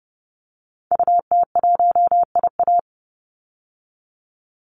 Morse code, heard as a keyed tone, about 30 wpm, 700 Hz. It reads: UT1IA
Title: Call Signs - 30wpm